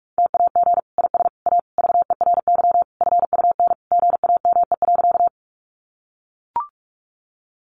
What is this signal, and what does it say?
Morse code, heard as a keyed tone, about 44 wpm, 700 Hz. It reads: TAG IS A VERY FUN GAME.
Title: Additional Easy Phrases 45wpm